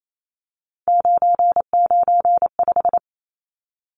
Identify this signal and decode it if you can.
Morse code, heard as a keyed tone, about 28 wpm, 700 Hz. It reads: 995
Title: Numbers - 3-Digit 28wpm